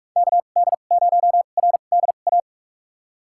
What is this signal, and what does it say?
Morse code, heard as a keyed tone, about 42 wpm, spaced as for 31 wpm, 700 Hz. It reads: KD0RDA
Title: USA Call Signs - 2x Character Spacing - 45wpm